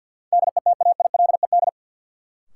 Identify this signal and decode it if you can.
Morse code, heard as a keyed tone, about 53 wpm, 700 Hz. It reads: DETAILED
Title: Sentences from Top 1500 Words - New Word Drill Alt - 50wpm